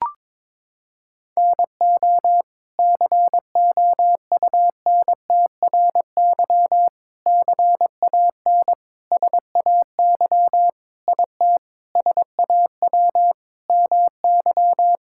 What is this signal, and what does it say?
Morse code, heard as a keyed tone, about 22 wpm, 700 Hz. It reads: NO COUNTRY CAN SAY IT SAW MY
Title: Sentences from Top 1000 Words - Review Part 5 - 22wpm